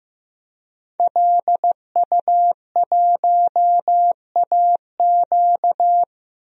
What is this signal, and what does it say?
Morse code, heard as a keyed tone, about 15 wpm, 700 Hz. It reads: LU1AQ
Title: DXCC-100 Call Signs - 5 Characters - Repeated 2x - 15wpm